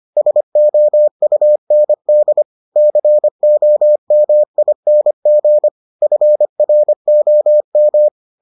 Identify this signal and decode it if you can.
Morse code, heard as a keyed tone, about 25 wpm, 600 Hz. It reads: SOUND COMING FROM